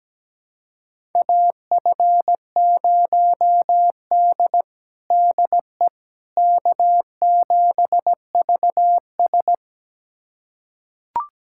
Morse code, heard as a keyed tone, about 17 wpm, 700 Hz. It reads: AF0D DE K7VS
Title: USA Callsign DE Callsign - 4 Characters - 17wpm